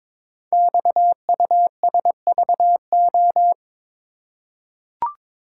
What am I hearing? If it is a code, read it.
Morse code, heard as a keyed tone, about 21 wpm, 700 Hz. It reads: XUSVO
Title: Code Groups - 5 Letter:Numbers - 22wpm